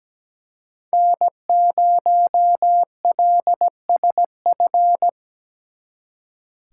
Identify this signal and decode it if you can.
Morse code, heard as a keyed tone, about 20 wpm, 700 Hz. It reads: N0LSF
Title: Speed Racing - USA 1x3 Call Signs 17wpm